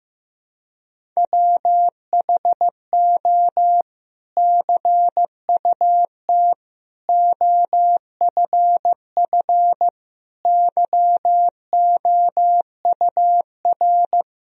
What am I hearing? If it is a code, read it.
Morse code, heard as a keyed tone, about 15 wpm, 700 Hz. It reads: WHO CUT OFF YOUR
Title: Sentences from Top 1000 Words - Review Part 1 - 15wpm